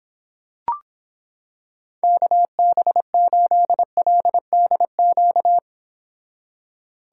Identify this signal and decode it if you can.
Morse code, heard as a keyed tone, about 26 wpm, 700 Hz. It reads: KB8LDQ